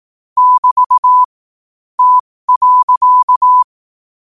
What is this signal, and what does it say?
Morse code, heard as a keyed tone, about 18 wpm, spaced as for 15 wpm, 1000 Hz. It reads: <BT> T.
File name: qotdfiles / 15wpm